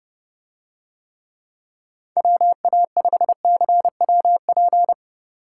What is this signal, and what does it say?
Morse code, heard as a keyed tone, about 30 wpm, 700 Hz. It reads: WA5CWP